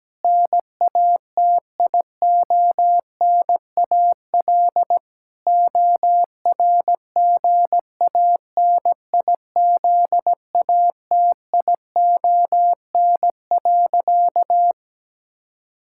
Morse code, heard as a keyed tone, about 17 wpm, 700 Hz. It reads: NATIONAL ORGANIZATION.
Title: Sentences from Top 1000 Words - Review Part 5 - 17wpm